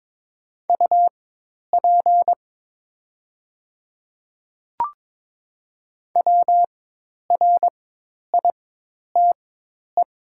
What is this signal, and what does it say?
Morse code, heard as a keyed tone, about 22 wpm, spaced as for 10 wpm, 700 Hz. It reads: UP WRITE
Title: 4x Character Spacing - Top 100 Words - 22wpm